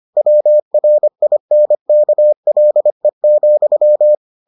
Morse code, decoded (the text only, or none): WRINKLE,